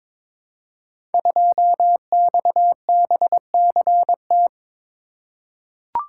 Morse code, heard as a keyed tone, about 22 wpm, 700 Hz. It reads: 2XBCT
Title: Code Groups - 5 Letter:Numbers - 22wpm